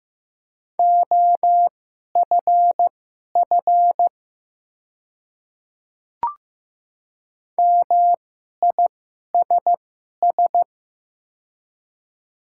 Morse code, heard as a keyed tone, about 15 wpm, spaced as for 11 wpm, 700 Hz. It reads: OFF MISS
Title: 2x Character Spacing - Top 500 Words - 15wpm